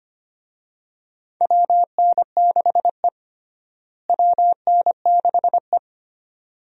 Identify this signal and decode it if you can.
Morse code, heard as a keyed tone, about 25 wpm, 700 Hz. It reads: WN6E WN6E
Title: USA Call Signs - 4 Characters - Repeated 2x - 25wpm